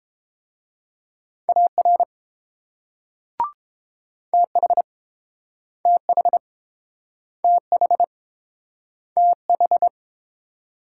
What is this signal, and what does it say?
Morse code, heard as a keyed tone, about 29 wpm, 700 Hz. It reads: AR TH TH TH TH